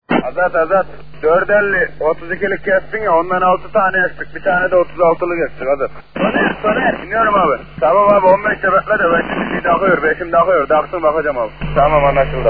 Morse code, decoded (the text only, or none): none